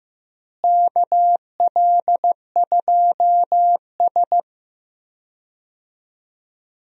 Morse code, heard as a keyed tone, about 15 wpm, 700 Hz. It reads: KL2S
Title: USA Call Signs - 4 Characters - Repeated 2x - 15wpm